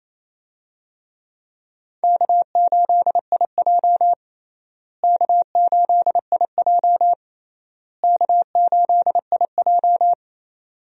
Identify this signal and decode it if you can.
Morse code, heard as a keyed tone, about 28 wpm, 700 Hz. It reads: K8IJ K8IJ K8IJ
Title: USA Call Signs - 4 Characters - Repeated 3x - 28wpm